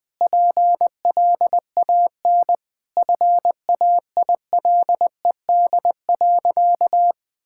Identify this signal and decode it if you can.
Morse code, heard as a keyed tone, about 20 wpm, 700 Hz. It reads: PLAN FAILED.
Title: Sentences with 9-Word Phrases - Spoken Beginning - 20wpm